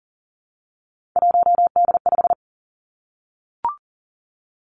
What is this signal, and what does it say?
Morse code, heard as a keyed tone, about 40 wpm, 700 Hz. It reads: 1D5
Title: Three Challenging Letter-Number Combinations 40wpm